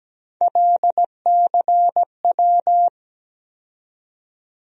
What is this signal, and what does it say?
Morse code, heard as a keyed tone, about 17 wpm, 700 Hz. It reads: LCW
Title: USA Call Sign Postfixes - 3 Letters 17wpm